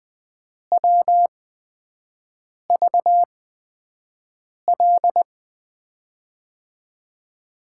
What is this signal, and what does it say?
Morse code, heard as a keyed tone, about 20 wpm, spaced as for 5 wpm, 700 Hz. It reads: WVL
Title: USA Call Signs - 6 Characters - 8x Character Spacing - 20wpm